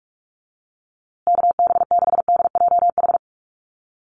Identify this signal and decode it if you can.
Morse code, heard as a keyed tone, about 44 wpm, 700 Hz. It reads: KB6DJH
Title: USA Call Signs Repeated 3x 45wpm